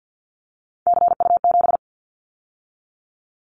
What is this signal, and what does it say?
Morse code, heard as a keyed tone, about 50 wpm, 700 Hz. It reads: CU7
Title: Callsign Prefixes for Top 50 DXCC - 50wpm